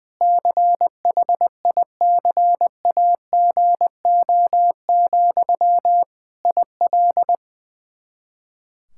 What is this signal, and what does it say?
Morse code, heard as a keyed tone, about 20 wpm, 700 Hz. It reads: CHICAGO, IL
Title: K1USN SST QSO Practice 20wpm